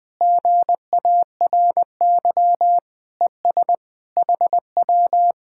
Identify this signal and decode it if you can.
Morse code, heard as a keyed tone, about 20 wpm, 700 Hz. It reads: GARY ES HW
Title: Speed-Racing Common QSO Phrases Encore - 20wpm